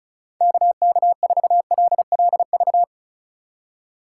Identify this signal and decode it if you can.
Morse code, heard as a keyed tone, about 36 wpm, 700 Hz. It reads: KK4LLV
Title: USA Call Signs - 6 Characters - 2x Character Spacing - 35wpm